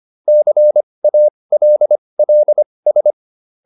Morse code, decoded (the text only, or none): CALLS